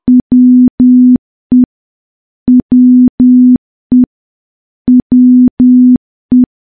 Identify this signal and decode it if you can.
Morse code, heard as a keyed tone, about 10 wpm, 250 Hz. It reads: WE WE WE
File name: morse / morse-training-10WPM